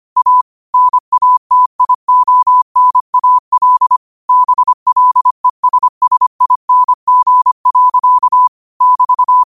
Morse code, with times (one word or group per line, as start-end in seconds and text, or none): 0.2-0.4 A
0.7-4.0 NATIONAL
4.3-8.5 BLESSING.
8.8-9.4 <BT>